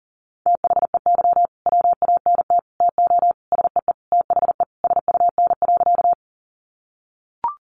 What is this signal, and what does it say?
Morse code, heard as a keyed tone, about 40 wpm, 700 Hz. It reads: THEY WANT TO SEE THE SUN.
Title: Easy Phrases 40wpm